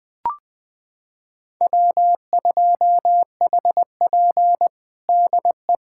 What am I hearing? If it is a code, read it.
Morse code, heard as a keyed tone, about 20 wpm, 700 Hz. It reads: W2HP DE
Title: USA Callsign DE Callsign - 4 Characters - 20wpm